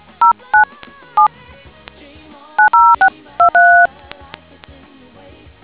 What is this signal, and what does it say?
DTMF, heard as telephone touch tones, tones of about 120 ms. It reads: 09 7 90623